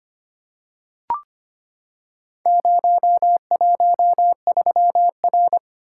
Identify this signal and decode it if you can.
Morse code, heard as a keyed tone, about 25 wpm, 700 Hz. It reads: 013R
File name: practice / 25wpm / Four Letter-Number Combinations - 25wpm